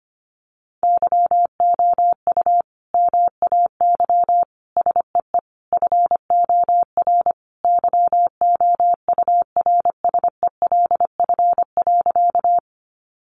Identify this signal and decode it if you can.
Morse code, heard as a keyed tone, about 25 wpm, 700 Hz. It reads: YOU MAY SEE FOR YOURSELF.
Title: Sentences from Top 600 Words 25wpm